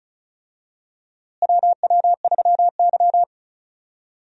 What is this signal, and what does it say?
Morse code, heard as a keyed tone, about 36 wpm, 700 Hz. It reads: WW3Y
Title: USA Call Signs - 4 Characters - Repeated 3x - 35wpm